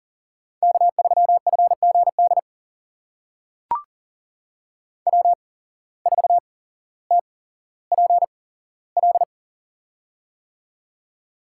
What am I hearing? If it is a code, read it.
Morse code, heard as a keyed tone, about 40 wpm, spaced as for 17 wpm, 700 Hz. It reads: K3FGD W4TPL
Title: USA Call Signs - 5 Characters - 8x Character Spacing - 40wpm